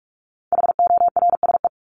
Morse code, heard as a keyed tone, about 44 wpm, 700 Hz. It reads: HORSE
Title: Top 300 Words 45wpm